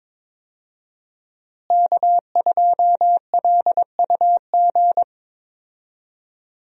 Morse code, heard as a keyed tone, about 22 wpm, 700 Hz. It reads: K2LUG
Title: USA Call Signs - 5 Characters - Repeated 3x - 22wpm